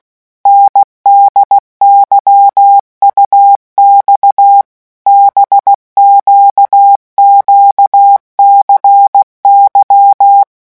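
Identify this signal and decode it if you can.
Morse code, heard as a keyed tone, about 16 wpm, 800 Hz. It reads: NDYUX BQQCY
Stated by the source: Morse letter-group revision drill at 15 words per minute.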